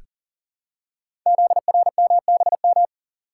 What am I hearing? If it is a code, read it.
Morse code, heard as a keyed tone, about 41 wpm, 700 Hz. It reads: ZRMBM
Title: Code Groups - 5 Letters - 40wpm